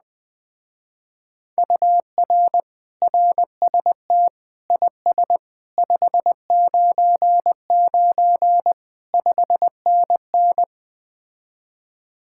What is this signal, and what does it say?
Morse code, heard as a keyed tone, about 20 wpm, 700 Hz. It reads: UR RST IS 599 5NN
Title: Speed-Racing Common QSO Phrases - 17wpm